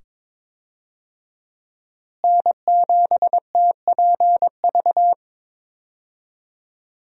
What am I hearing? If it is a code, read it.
Morse code, heard as a keyed tone, about 22 wpm, 700 Hz. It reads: N7TPV
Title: USA Call Signs - 5 Characters - Repeated 2x - 22wpm